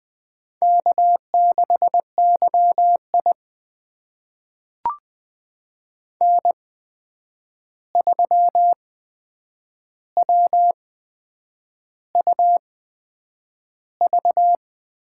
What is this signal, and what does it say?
Morse code, heard as a keyed tone, about 20 wpm, spaced as for 8 wpm, 700 Hz. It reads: K6YI N3WUV
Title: USA Call Signs - 8x Character Spacing - 20wpm